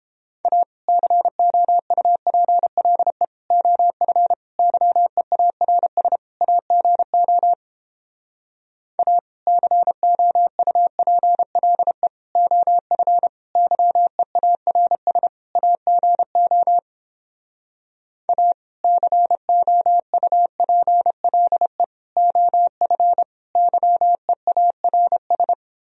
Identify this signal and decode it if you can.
Morse code, heard as a keyed tone, about 31 wpm, 700 Hz. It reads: A COUPLE OF YEARS AGO A COUPLE OF YEARS AGO A COUPLE OF YEARS